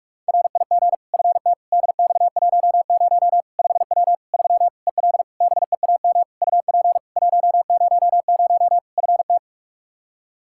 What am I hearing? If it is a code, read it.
Morse code, heard as a keyed tone, about 42 wpm, 700 Hz. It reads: RIG FT DX10 5W 3 EL BEAM UP 100 FT
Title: Practice QSO - Exchange 2 - Info - 45wpm